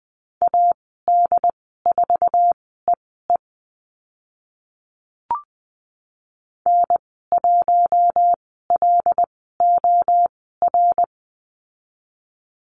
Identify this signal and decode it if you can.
Morse code, heard as a keyed tone, about 20 wpm, spaced as for 14 wpm, 700 Hz. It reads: AD4EE N1LOR
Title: 076 Lesson 13g - Letter L - Call Signs 20wpm